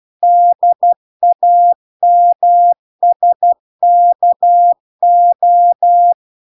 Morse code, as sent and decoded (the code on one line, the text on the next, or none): -.. .- -- ... -.- ---
DAMSKO